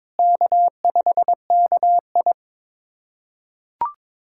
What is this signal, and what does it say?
Morse code, heard as a keyed tone, about 22 wpm, 700 Hz. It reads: K5KI
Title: USA Callsign DE Callsign - 4 Characters - 22wpm